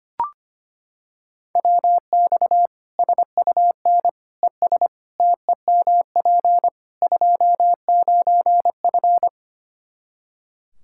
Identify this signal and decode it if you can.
Morse code, heard as a keyed tone, about 25 wpm, 700 Hz. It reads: WX SUN ES TEMP 29F
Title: K1USN SST QSO Practice 25wpm